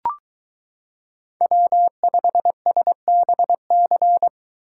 Morse code, heard as a keyed tone, about 22 wpm, 700 Hz. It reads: W5SBC